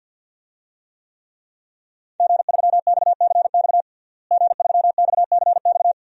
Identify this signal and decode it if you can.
Morse code, heard as a keyed tone, about 51 wpm, 700 Hz. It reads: G3XCX G3XCX
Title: DXCC-100 Call Signs - 5 Characters - Repeated 3x - 50wpm